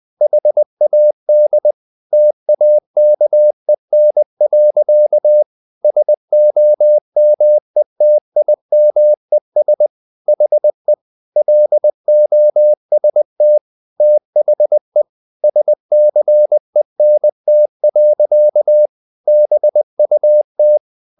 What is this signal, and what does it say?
Morse code, heard as a keyed tone, about 20 wpm, 600 Hz. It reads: HAD TAKEN. SOMETIMES HE LOST THE SCENT. BUT
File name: the old man and the sea part 2 20wpm